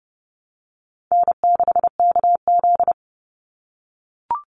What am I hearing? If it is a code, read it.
Morse code, heard as a keyed tone, about 30 wpm, 700 Hz. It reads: N6KZ
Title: NA SOTA Chaser Callsigns 30wpm